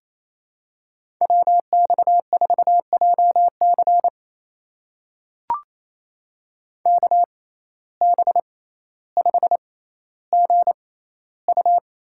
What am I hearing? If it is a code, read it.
Morse code, heard as a keyed tone, about 28 wpm, spaced as for 14 wpm, 700 Hz. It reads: WX4JC KB5GU
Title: USA Call Signs - 6x Character Spacing - 28wpm